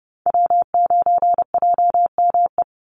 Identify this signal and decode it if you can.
Morse code, heard as a keyed tone, about 29 wpm, 700 Hz. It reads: W9JME
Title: USA Call Signs 30wpm